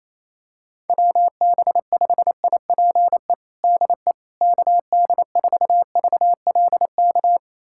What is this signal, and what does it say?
Morse code, heard as a keyed tone, about 28 wpm, 700 Hz. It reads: WB5IPE DE KD4VLK
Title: USA Callsign DE Callsign 28wpm